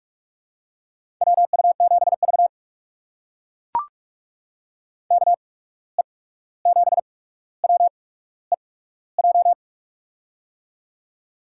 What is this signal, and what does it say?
Morse code, heard as a keyed tone, about 44 wpm, spaced as for 17 wpm, 700 Hz. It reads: WU7V KE7WEJ
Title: USA Call Signs - 8x Character Spacing - 45wpm